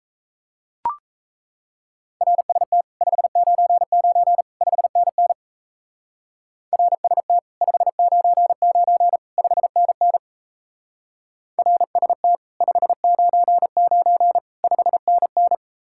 Morse code, heard as a keyed tone, about 33 wpm, 700 Hz. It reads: RST 599 5NN RST 599 5NN RST 599 5NN
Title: Speed-Racing Common QSO Phrases Encore - 28wpm